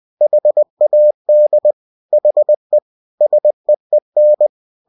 Morse code, heard as a keyed tone, about 20 wpm, 600 Hz. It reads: HAD HE SEEN
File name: cw-books / winnie-the-pooh / winnie-20wpm-page-0090